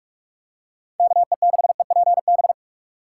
Morse code, heard as a keyed tone, about 44 wpm, 700 Hz. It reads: KE6EPB